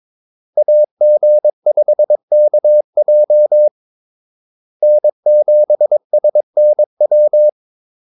Morse code, heard as a keyed tone, about 22 wpm, 600 Hz. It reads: AG5KJ N7SNW
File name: Callsigns-22wpm